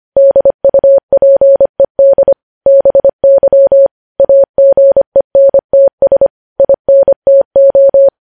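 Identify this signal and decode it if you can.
Morse code, heard as a keyed tone, about 25 wpm, 550 Hz. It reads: DUPED BY AGENTS INTO
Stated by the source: News headlines sent at 25.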